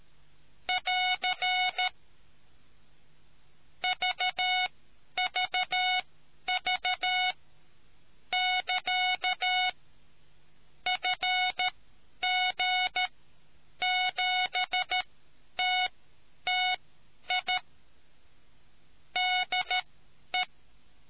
Morse code, heard as a keyed tone, about 13 wpm, spaced as for 9 wpm, 740 Hz. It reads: <AR> VVV <KA> FG7TTI DE